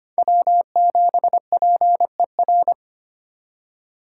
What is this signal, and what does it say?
Morse code, heard as a keyed tone, about 32 wpm, 700 Hz. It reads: W7PER